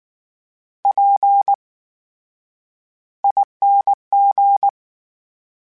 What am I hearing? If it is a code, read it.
Morse code, heard as a keyed tone, about 19 wpm, spaced as for 9 wpm, 800 Hz. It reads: PING
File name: Special INGs / Special INGs-mp3s / VerySlowToFaster Ing @18